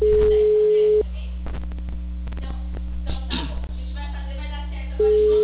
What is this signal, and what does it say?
Ringback tone (Europe, 425 Hz), with speech.
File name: out- -2064- - - .30554